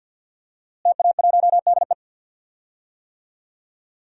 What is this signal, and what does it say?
Morse code, heard as a keyed tone, about 51 wpm, 700 Hz. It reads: TA1DE
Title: DXCC-100 Call Signs - 5 Characters - Repeated 2x - 50wpm